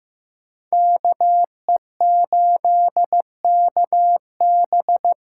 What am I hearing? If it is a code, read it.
Morse code, heard as a keyed tone, about 15 wpm, 700 Hz. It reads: KE8KB
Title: USA Call Signs - 6 Characters - 8x Character Spacing - 15wpm